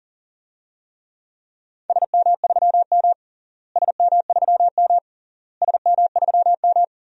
Morse code, heard as a keyed tone, about 40 wpm, 700 Hz. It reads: SM3M SM3M SM3M
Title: DXCC-100 Call Signs - 4 Characters - Repeated 3x - 40wpm